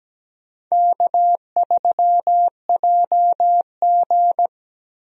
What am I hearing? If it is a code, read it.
Morse code, heard as a keyed tone, about 17 wpm, 700 Hz. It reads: K3JG
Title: USA Call Signs - 4 Characters - Repeated 3x - 17wpm